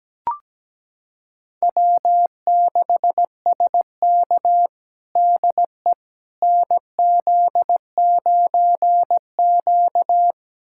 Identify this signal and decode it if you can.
Morse code, heard as a keyed tone, about 17 wpm, 700 Hz. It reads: W6SK DE NZ9Q
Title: USA Callsign DE Callsign - 4 Characters - 17wpm